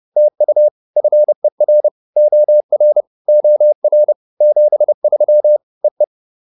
Morse code, heard as a keyed tone, about 30 wpm, 600 Hz. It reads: TU FER OR OR 73 EE